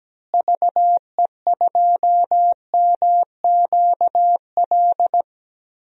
Morse code, heard as a keyed tone, about 17 wpm, 700 Hz. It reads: VE2MQL
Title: DXCC-100 Call Signs - 6 Characters - Repeated 3x - 17wpm